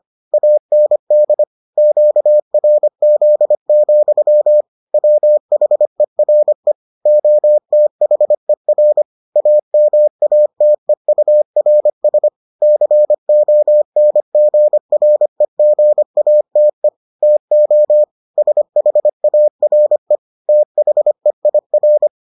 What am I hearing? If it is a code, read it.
Morse code, heard as a keyed tone, about 25 wpm, 600 Hz. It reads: AND QRZ, WHERE OTHER AMATEURS CONGREGATE TO SHARE THEIR